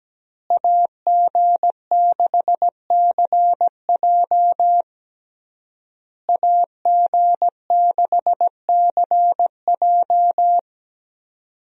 Morse code, heard as a keyed tone, about 17 wpm, 700 Hz. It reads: AG6CJ AG6CJ
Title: USA Call Signs - 5 Characters - Repeated 3x - 17wpm